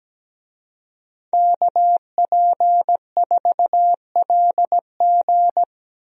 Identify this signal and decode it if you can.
Morse code, heard as a keyed tone, about 17 wpm, 700 Hz. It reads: KP4LG
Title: DXCC-100 Call Signs - 5 Characters - Repeated 2x - 17wpm